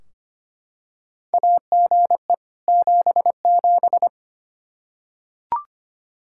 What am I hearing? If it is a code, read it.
Morse code, heard as a keyed tone, about 25 wpm, 700 Hz. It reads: AGE 77
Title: Practice QSO - Exchange 3 - More Info - 25wpm